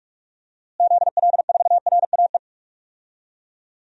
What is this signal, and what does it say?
Morse code, heard as a keyed tone, about 46 wpm, 700 Hz. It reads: ZL4RAE
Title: DXCC-100 Call Signs - 6 Characters - Repeated 3x - 45wpm